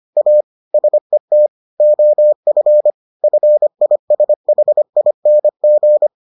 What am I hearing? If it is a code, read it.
Morse code, heard as a keyed tone, about 25 wpm, 600 Hz. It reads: A SET OF FISHING